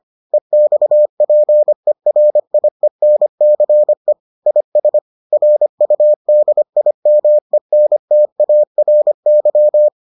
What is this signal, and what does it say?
Morse code, heard as a keyed tone, about 25 wpm, 600 Hz. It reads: EXPERIENCE IS RUDIMENTARY